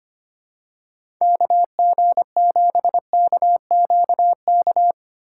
Morse code, heard as a keyed tone, about 25 wpm, 700 Hz. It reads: KG7KQK
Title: USA Call Signs - 6 Characters - 3x Character Spacing - 25wpm